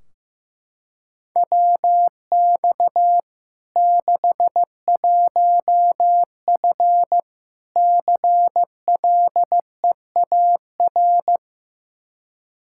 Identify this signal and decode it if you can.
Morse code, heard as a keyed tone, about 15 wpm, 700 Hz. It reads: WX 61F CLEAR
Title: K1USN SST QSO Practice 15wpm